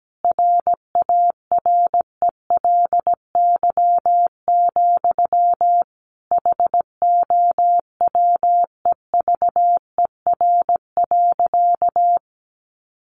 Morse code, heard as a keyed tone, about 17 wpm, 700 Hz. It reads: RARELY, HOWEVER.
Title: Sentences from Top 1400 Words 17wpm